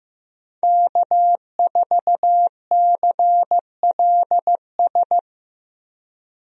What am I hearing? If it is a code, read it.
Morse code, heard as a keyed tone, about 15 wpm, 700 Hz. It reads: K4CLS
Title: USA Call Signs - 5 Characters - Repeated 2x - 15wpm